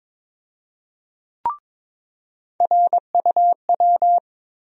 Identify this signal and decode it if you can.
Morse code, heard as a keyed tone, about 22 wpm, 700 Hz. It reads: RUW
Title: USA Call Sign Postfixes - 3 Letters 22wpm